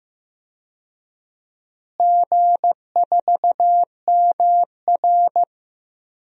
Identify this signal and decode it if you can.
Morse code, heard as a keyed tone, about 15 wpm, 700 Hz. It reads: G4MR
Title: DXCC-100 Call Signs - 4 Characters - Repeated 3x - 15wpm